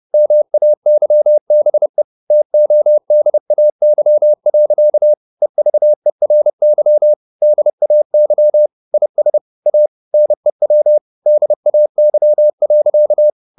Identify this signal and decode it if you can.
Morse code, heard as a keyed tone, about 30 wpm, 600 Hz. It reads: MAYBE TODAY. EVERY DAY IS A NEW DAY.
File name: the old man and the sea / the old man and the sea 1 30wpm